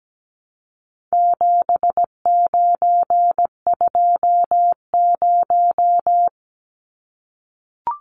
Morse code, heard as a keyed tone, about 17 wpm, 700 Hz. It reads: 7920
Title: Numbers - 4-Digit 17wpm